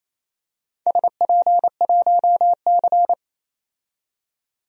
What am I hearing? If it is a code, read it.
Morse code, heard as a keyed tone, about 28 wpm, 700 Hz. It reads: SP1C
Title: DXCC-100 Call Signs - 4 Characters - 28wpm